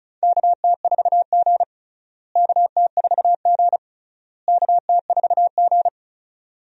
Morse code, heard as a keyed tone, about 36 wpm, 700 Hz. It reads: KT4G KT4G KT4G
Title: USA Call Signs - 4 Characters - Repeated 3x - 35wpm